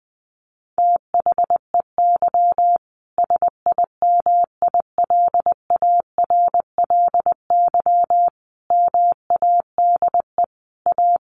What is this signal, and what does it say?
Morse code, heard as a keyed tone, about 20 wpm, 700 Hz. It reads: THEY SIMILARLY MADE A
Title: Sentences from Top 1400 Words 20wpm